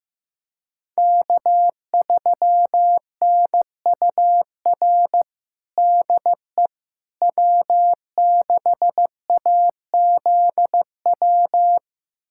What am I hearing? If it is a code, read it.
Morse code, heard as a keyed tone, about 15 wpm, 700 Hz. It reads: K3NUR DE W6AZW
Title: USA Callsign DE Callsign - 5 Characters - 15wpm